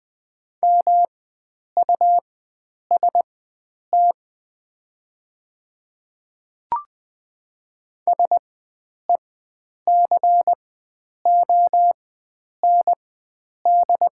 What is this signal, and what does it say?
Morse code, heard as a keyed tone, about 20 wpm, spaced as for 9 wpm, 700 Hz. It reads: MUST SECOND
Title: 4x Character Spacing - Top 300 Words - 20wpm